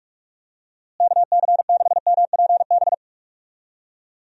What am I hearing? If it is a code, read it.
Morse code, heard as a keyed tone, about 43 wpm, 700 Hz. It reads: KC6MPB
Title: USA Call Signs - 6 Characters - Repeated 3x - 45wpm